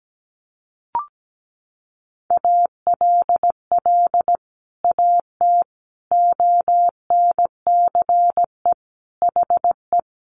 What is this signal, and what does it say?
Morse code, heard as a keyed tone, about 17 wpm, 700 Hz. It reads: ALL AT ONCE HE
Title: Sentences from Top 1100 Words 17wpm